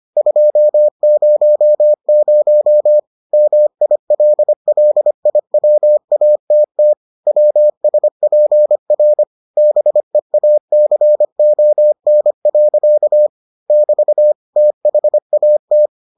Morse code, heard as a keyed tone, about 25 wpm, 600 Hz. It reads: 200 MILLIWATT WSPR BEACON. <BT> THAT